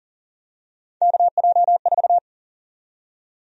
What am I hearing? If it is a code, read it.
Morse code, heard as a keyed tone, about 40 wpm, 700 Hz. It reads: KJ4
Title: USA Call Sign Prefixes - Repeated 2x - 40wpm